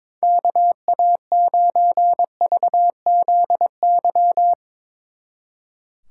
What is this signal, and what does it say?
Morse code, heard as a keyed tone, about 23 wpm, 700 Hz. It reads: KA9VZY
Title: Speed Racing - USA 2x3 Call Signs 22wpm